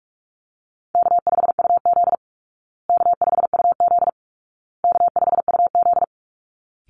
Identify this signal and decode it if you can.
Morse code, heard as a keyed tone, about 44 wpm, 700 Hz. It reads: K5UZ K5UZ K5UZ
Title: Easy Call Signs Repeated 3x - 45wpm